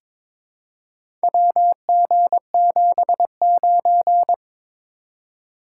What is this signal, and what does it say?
Morse code, heard as a keyed tone, about 22 wpm, 700 Hz. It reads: WG79
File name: practice / 22wpm / Four Letter-Number Combinations - 22wpm